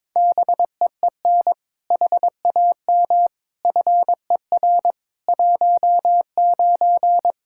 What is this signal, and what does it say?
Morse code, heard as a keyed tone, about 22 wpm, 700 Hz. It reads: BEEN HAM FER 19
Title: Practice QSO - Exchange 3 - More Info - 22wpm